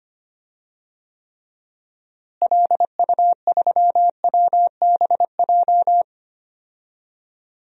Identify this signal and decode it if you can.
Morse code, heard as a keyed tone, about 25 wpm, 700 Hz. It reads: LU3WBJ
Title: DXCC-100 Call Signs - 6 Characters - Repeated 3x - 25wpm